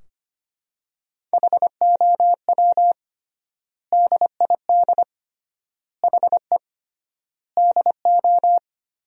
Morse code, heard as a keyed tone, about 25 wpm, 700 Hz. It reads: HOW DID HE DO
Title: 3x Word-Spacing - Sentences from Top 100 Words - 25wpm